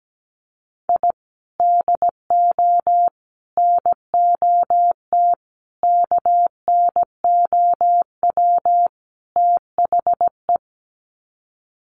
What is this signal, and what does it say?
Morse code, heard as a keyed tone, about 17 wpm, 700 Hz. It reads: I DO NOT KNOW THE
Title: Sets of 5 Words 17wpm